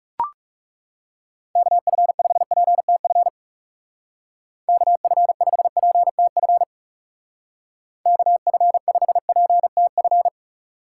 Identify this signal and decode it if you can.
Morse code, heard as a keyed tone, about 39 wpm, 700 Hz. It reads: KF5PTF KF5PTF KF5PTF